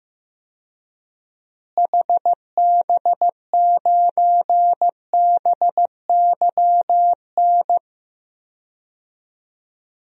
Morse code, heard as a keyed tone, about 15 wpm, 700 Hz. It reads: HB9BYN
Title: DXCC-100 Call Signs - 6 Characters - Repeated 2x - 15wpm